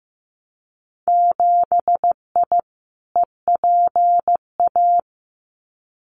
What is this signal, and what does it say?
Morse code, heard as a keyed tone, about 15 wpm, 700 Hz. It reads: 7I EPA
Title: Winter Field Day Exchanges Encore 15wpm